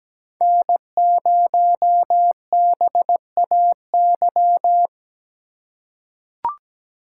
Morse code, heard as a keyed tone, about 17 wpm, 700 Hz. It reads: N0BAY
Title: USA Call Signs - 5 Characters - 4x Character Spacing - 17wpm